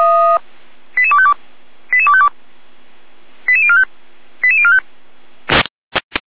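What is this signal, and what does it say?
Five-tone selective call (ZVEI), heard as quick runs of five tones, tones of about 70 ms. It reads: 88252 88252 88045 88045
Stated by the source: ZVEI five-tone call received by SDR, from feuerwehr-orlamuende.org fire brigade audio archive.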